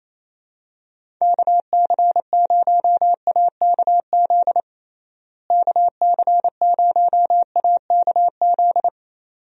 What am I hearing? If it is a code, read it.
Morse code, heard as a keyed tone, about 28 wpm, 700 Hz. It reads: KC0AKZ KC0AKZ
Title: USA Call Signs - 6 Characters - Repeated 2x - 28wpm